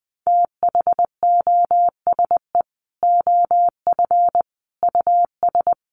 Morse code, heard as a keyed tone, about 19 wpm, 700 Hz. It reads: THOSE OF US
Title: Sets of 5 Words Encore2 20wpm